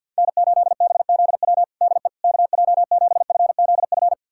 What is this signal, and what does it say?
Morse code, heard as a keyed tone, about 52 wpm, 700 Hz. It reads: N8BZW DE KJ7FZF
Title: USA Callsign DE Callsign 50wpm